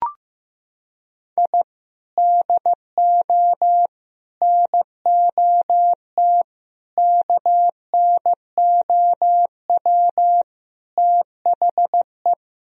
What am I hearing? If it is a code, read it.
Morse code, heard as a keyed tone, about 15 wpm, 700 Hz. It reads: I DO NOT KNOW THE
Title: Sentences from Top 1000 Words - Review Part 3 - 15wpm